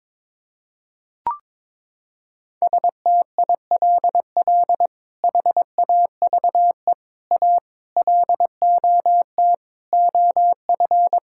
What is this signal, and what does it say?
Morse code, heard as a keyed tone, about 22 wpm, 700 Hz. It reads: STILL HAVE A LOT OF
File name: practice / 22wpm / Sets of 8 Words - 22wpm